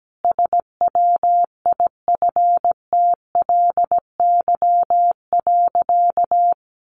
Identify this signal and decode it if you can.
Morse code, heard as a keyed tone, about 17 wpm, 700 Hz. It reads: SWIFTLY.
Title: A Sound Of Thunder 17wpm